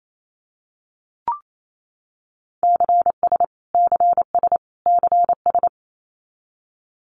Morse code, heard as a keyed tone, about 28 wpm, 700 Hz. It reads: CS CS CS
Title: Abbreviations Repeated 3x - 28wpm